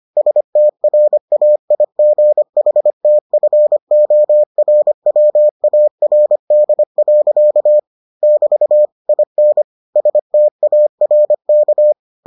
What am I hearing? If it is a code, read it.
Morse code, heard as a keyed tone, about 25 wpm, 600 Hz. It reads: STRAIGHTFORWARD. <BT> IN STARK